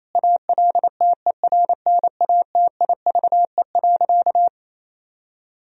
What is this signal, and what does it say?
Morse code, heard as a keyed tone, about 28 wpm, 700 Hz. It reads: ALTERNATIVE.
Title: Sentences from Top 1000 Words - Review Part 4 - 28wpm